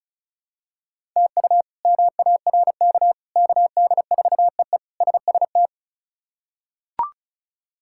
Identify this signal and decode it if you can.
Morse code, heard as a keyed tone, about 36 wpm, 700 Hz. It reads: TU MARK KD4EE SST
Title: K1USN SST QSO Practice - 35wpm